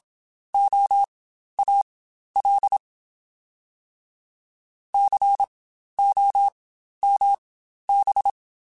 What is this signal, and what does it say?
Morse code, heard as a keyed tone, about 27 wpm, spaced as for 12 wpm, 800 Hz. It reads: OAL COMB